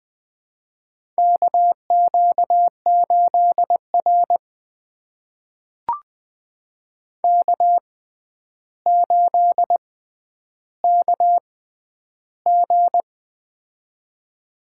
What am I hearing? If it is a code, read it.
Morse code, heard as a keyed tone, about 20 wpm, spaced as for 10 wpm, 700 Hz. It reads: KQ8R K8KG
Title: USA Call Signs - 4 Characters - 6x Character Spacing - 20wpm